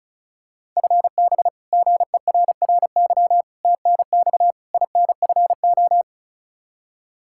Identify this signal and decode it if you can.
Morse code, heard as a keyed tone, about 36 wpm, 700 Hz. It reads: FB GERRY TNX INFO
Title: Practice QSO - Exchange 3 - More Info - 35wpm